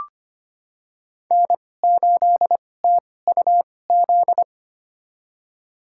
Morse code, heard as a keyed tone, about 25 wpm, spaced as for 18 wpm, 700 Hz. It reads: N8TUZ
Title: USA Call Signs - 2x Character Spacing - 25wpm